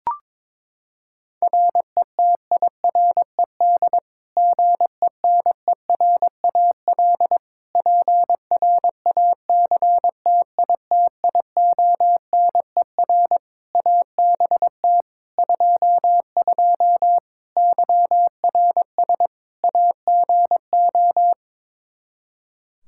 Morse code, heard as a keyed tone, about 22 wpm, 700 Hz. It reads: RETIRED GENERAL PRACTITIONER ABT 22 YRS AGO
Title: Practice QSO - Exchange 3 - More Info - 22wpm